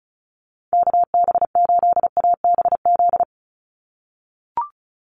Morse code, heard as a keyed tone, about 34 wpm, 700 Hz. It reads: KB8ABZ
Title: USA Call Signs 35wpm